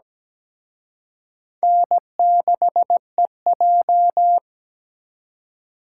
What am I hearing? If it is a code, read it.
Morse code, heard as a keyed tone, about 20 wpm, 700 Hz. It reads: N6EJ
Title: Speed Racing - USA Call Signs 17wpm